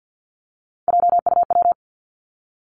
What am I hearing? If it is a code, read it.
Morse code, heard as a keyed tone, about 50 wpm, 700 Hz. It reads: JUW
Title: Three Challenging Letter-Number Combinations 50wpm